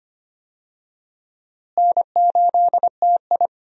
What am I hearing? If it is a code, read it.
Morse code, heard as a keyed tone, about 25 wpm, 700 Hz. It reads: N8TI
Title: USA Call Signs - 4 Characters - Repeated 2x - 25wpm